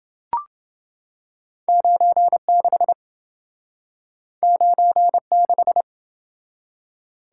Morse code, heard as a keyed tone, about 28 wpm, 700 Hz. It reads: 96 96